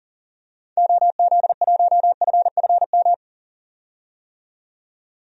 Morse code, heard as a keyed tone, about 40 wpm, 700 Hz. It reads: OZ1FFM
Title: DXCC-100 Call Signs - 6 Characters - 40wpm